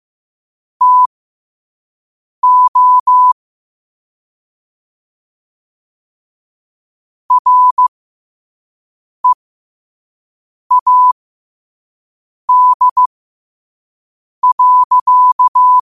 Morse code, heard as a keyed tone, about 15 wpm, spaced as for 5 wpm, 1000 Hz. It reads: TO READ.
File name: qotdfiles / 5wpm